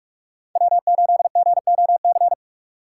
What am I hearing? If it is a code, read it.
Morse code, heard as a keyed tone, about 44 wpm, 700 Hz. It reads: W 8 G O C
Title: USA Call Signs - 5 Characters - 6x Character Spacing - 45wpm